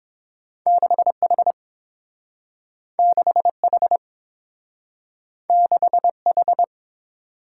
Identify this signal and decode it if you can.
Morse code, heard as a keyed tone, about 30 wpm, 700 Hz. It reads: 6H 6H 6H